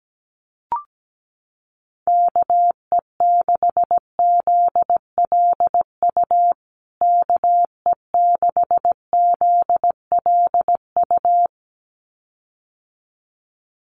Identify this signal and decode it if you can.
Morse code, heard as a keyed tone, about 17 wpm, 700 Hz. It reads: KE6ZLU KE6ZLU
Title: USA Call Signs Repeated 2x 17wpm